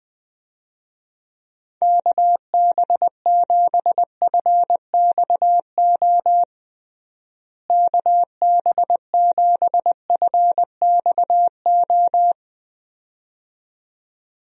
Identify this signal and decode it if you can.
Morse code, heard as a keyed tone, about 20 wpm, 700 Hz. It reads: KB7FXO KB7FXO
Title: USA Call Signs - 6 Characters - Repeated 2x - 20wpm